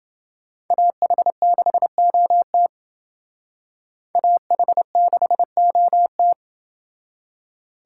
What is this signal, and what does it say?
Morse code, heard as a keyed tone, about 28 wpm, 700 Hz. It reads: AH6OT AH6OT